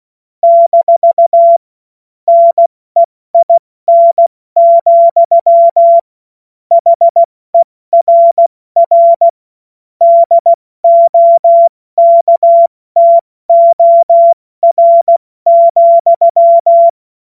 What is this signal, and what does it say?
Morse code, heard as a keyed tone, about 16 wpm, 680 Hz. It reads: - NEIN, HERR DOKTOR,
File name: Morsekurs / Witze / Witz 26 März 2024 70 F80